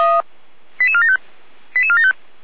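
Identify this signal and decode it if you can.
Five-tone selective call (ZVEI), heard as quick runs of five tones, tones of about 70 ms. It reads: 88475 88475
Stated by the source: ZVEI five-tone call received by SDR, from the feuerwehr-orlamuende.org archive.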